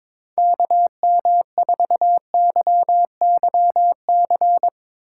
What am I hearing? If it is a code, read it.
Morse code, heard as a keyed tone, about 22 wpm, 700 Hz. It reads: KM4YYC
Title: Speed Racing - USA Call Signs 22wpm